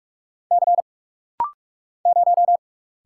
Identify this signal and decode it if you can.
Morse code, heard as a keyed tone, about 46 wpm, 700 Hz. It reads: C 0
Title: Single Letter-Number Alt - Rapid-Fire 45wpm